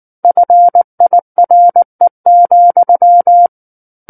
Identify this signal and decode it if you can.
Morse code, heard as a keyed tone, about 19 wpm, 700 Hz. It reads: FIRE,
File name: Book One / 10-Chapter 10 - 19 WPM